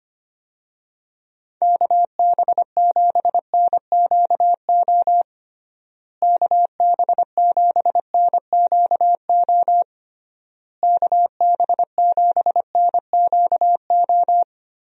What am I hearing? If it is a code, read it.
Morse code, heard as a keyed tone, about 25 wpm, 700 Hz. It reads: KB7NQO KB7NQO KB7NQO
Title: USA Call Signs - 6 Characters - Repeated 3x - 25wpm